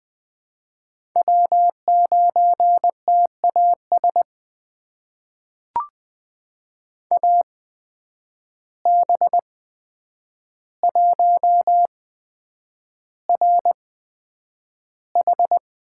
Morse code, heard as a keyed tone, about 20 wpm, spaced as for 9 wpm, 700 Hz. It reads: W9TAS AB1RH
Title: USA Call Signs - 5 Characters - 8x Character Spacing - 20wpm